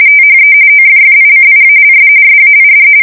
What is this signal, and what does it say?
RTTY, about 45.45 baud. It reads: THE OFFICIALS SAID
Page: 170 Shift 2125/2295